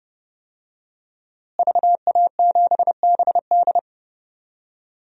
Morse code, heard as a keyed tone, about 30 wpm, 700 Hz. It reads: VA7BD
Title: DXCC-100 Call Signs - 5 Characters - Repeated 2x - 30wpm